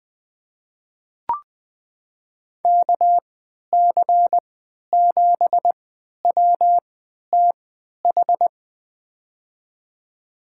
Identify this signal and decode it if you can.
Morse code, heard as a keyed tone, about 20 wpm, spaced as for 11 wpm, 700 Hz. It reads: KC7WTH
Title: USA Call Signs - 6 Characters - 3x Character Spacing - 20wpm